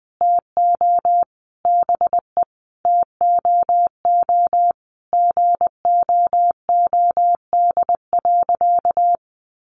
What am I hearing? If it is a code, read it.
Morse code, heard as a keyed tone, about 20 wpm, 700 Hz. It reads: TO BE TOO GOOD.
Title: Sentences from Top 1200 Words 20wpm